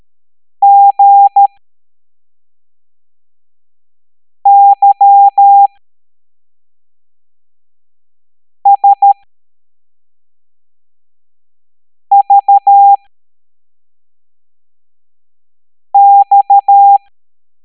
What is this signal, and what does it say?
Morse code, heard as a keyed tone, about 13 wpm, spaced as for 3 wpm, 800 Hz. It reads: GYSVX
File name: hamdata / L05-EX01-SVY